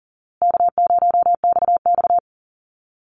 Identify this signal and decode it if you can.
Morse code, heard as a keyed tone, about 40 wpm, 700 Hz. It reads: K0XX
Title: Easy Call Signs Repeated 3x - 40wpm